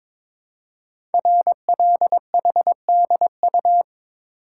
Morse code, heard as a keyed tone, about 22 wpm, 700 Hz. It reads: RLHDU
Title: Code Groups - 5 Letters - 22wpm